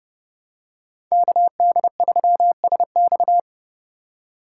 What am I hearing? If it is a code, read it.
Morse code, heard as a keyed tone, about 30 wpm, 700 Hz. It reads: KD3SX
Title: USA Call Signs - 5 Characters - 4x Character Spacing - 30wpm